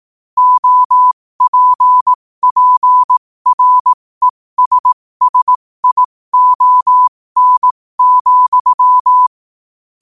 Morse code, heard as a keyed tone, about 18 wpm, spaced as for 15 wpm, 1000 Hz. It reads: OPPRESSION,
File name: qotdfiles / 15wpm